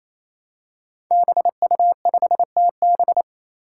Morse code, heard as a keyed tone, about 27 wpm, 700 Hz. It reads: BU5TB
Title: Five Challenging Letter-Number Combinations 28wpm